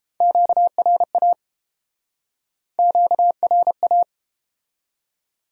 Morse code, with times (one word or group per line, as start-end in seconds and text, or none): 0.2-1.3 QRA
2.8-4.0 QRA